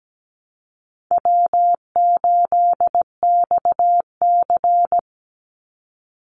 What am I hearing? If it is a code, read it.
Morse code, heard as a keyed tone, about 17 wpm, 700 Hz. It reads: W8XC
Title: Easy Call Signs Repeated 2x - 17wpm